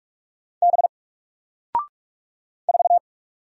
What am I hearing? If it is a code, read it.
Morse code, heard as a keyed tone, about 44 wpm, 700 Hz. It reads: B 4
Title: Single Letter-Number 45wpm